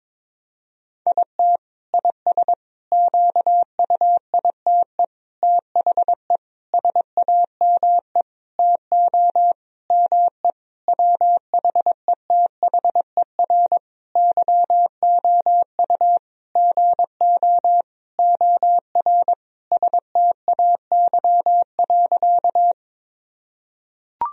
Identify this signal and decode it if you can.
Morse code, heard as a keyed tone, about 22 wpm, 700 Hz. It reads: IT IS QUITE THE SAME TO ME WHETHER YOU GO OR STAY.
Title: Sentences from Top 1000 Words - Review Part 3 - 22wpm